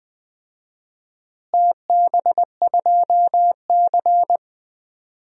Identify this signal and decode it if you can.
Morse code, heard as a keyed tone, about 20 wpm, 700 Hz. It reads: TB2C
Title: DXCC-100 Call Signs - 4 Characters - Repeated 2x - 20wpm